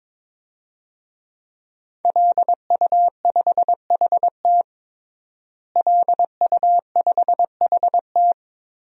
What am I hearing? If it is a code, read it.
Morse code, heard as a keyed tone, about 22 wpm, 700 Hz. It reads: LU5HT LU5HT
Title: DXCC-100 Call Signs - 5 Characters - Repeated 2x - 22wpm